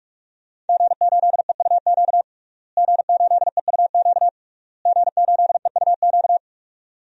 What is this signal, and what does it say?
Morse code, heard as a keyed tone, about 47 wpm, 700 Hz. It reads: G8EUQ G8EUQ G8EUQ
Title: DXCC-100 Call Signs - 5 Characters - Repeated 3x - 45wpm